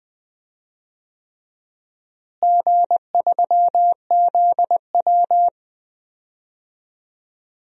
Morse code, heard as a keyed tone, about 20 wpm, 700 Hz. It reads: G3ZW
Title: DXCC-100 Call Signs - 4 Characters - Repeated 3x - 20wpm